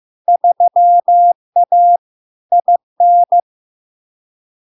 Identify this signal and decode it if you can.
Morse code, heard as a keyed tone, about 15 wpm, 700 Hz. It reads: 3A IN
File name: LARK-FD-2022-Log-in-CW-15wpm-700-Hz